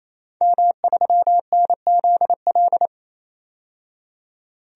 Morse code, heard as a keyed tone, about 28 wpm, 700 Hz. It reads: M3NZL
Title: DXCC-100 Call Signs - 5 Characters - 28wpm